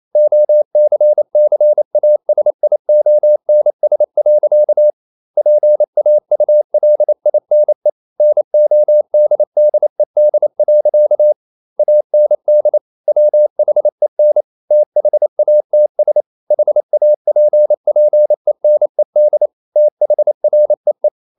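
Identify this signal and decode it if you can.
Morse code, heard as a keyed tone, about 28 wpm, 600 Hz. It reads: OCCASIONS. PAULINE NODDED. AND WHEN THATS HAPPENED THREE
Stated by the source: Morse code practice at 28 wpm, text of Ken Follett's novel Never in English, from CW med Gnister.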